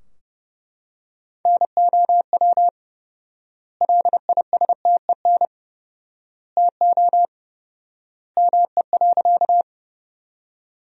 Morse code, heard as a keyed tone, about 30 wpm, 700 Hz. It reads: NOW LISTEN TO ME.
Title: 4x Word-Spacing - Sentences from Top 400 Words - 30wpm